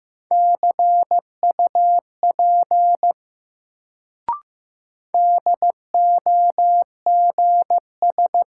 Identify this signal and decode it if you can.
Morse code, heard as a keyed tone, about 15 wpm, 700 Hz. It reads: CUP DOGS
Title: Sentences from Top 1600 Words - New Word Drill - 15wpm